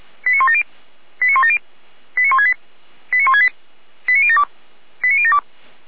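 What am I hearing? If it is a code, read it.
Five-tone selective call (ZVEI), heard as quick runs of five tones, tones of about 80 ms. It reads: 89170 89170 89168 89168 89072 89072
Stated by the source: ZVEI five-tone selective call received on an SDR.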